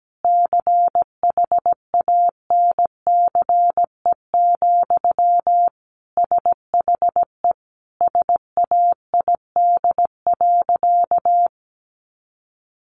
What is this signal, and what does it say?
Morse code, heard as a keyed tone, about 17 wpm, 700 Hz. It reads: CHANCE, SHE SAID.
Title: Sentences from Top 800 Words 17wpm